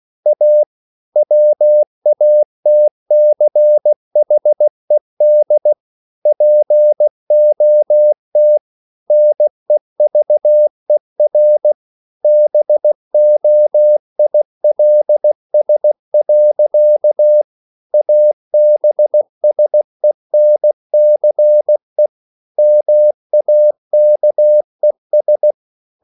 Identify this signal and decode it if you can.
Morse code, heard as a keyed tone, about 16 wpm, 600 Hz. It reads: A WATCHED POT NEVER BOILS. ABSENCE MAKES
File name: CW / Practice / EnglishSayings16